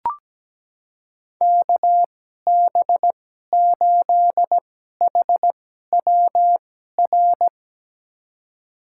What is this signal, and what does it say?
Morse code, heard as a keyed tone, about 17 wpm, spaced as for 12 wpm, 700 Hz. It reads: KB8HWR
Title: USA Call Signs - 6 Characters - 2x Character Spacing - 17wpm